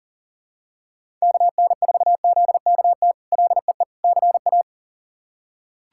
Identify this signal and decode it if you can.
Morse code, heard as a keyed tone, about 40 wpm, 700 Hz. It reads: KN4ZKT LEE CA
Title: K1USN SST QSO Practice - 40wpm